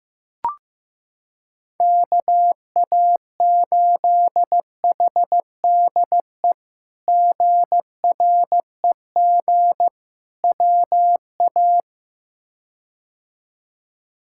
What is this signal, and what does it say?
Morse code, heard as a keyed tone, about 15 wpm, 700 Hz. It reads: KA8HDE GREG WA
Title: K1USN SST QSO Practice - 15wpm